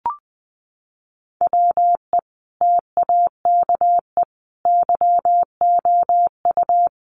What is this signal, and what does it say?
Morse code, heard as a keyed tone, about 20 wpm, 700 Hz. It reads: WE TAKE YOU
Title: A Sound Of Thunder 20wpm